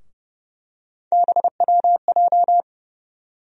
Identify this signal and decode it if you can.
Morse code, heard as a keyed tone, about 30 wpm, 700 Hz. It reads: BWJ
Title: USA Call Sign Postfixes - 3 Letters 30wpm